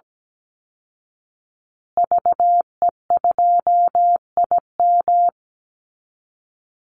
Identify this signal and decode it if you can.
Morse code, heard as a keyed tone, about 17 wpm, 700 Hz. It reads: VE2IM
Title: Call Signs Repeated 2x - 17wpm